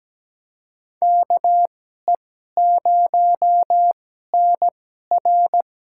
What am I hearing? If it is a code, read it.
Morse code, heard as a keyed tone, about 17 wpm, spaced as for 12 wpm, 700 Hz. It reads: KE0NR
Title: USA Call Signs - 2x Character Spacing - 17wpm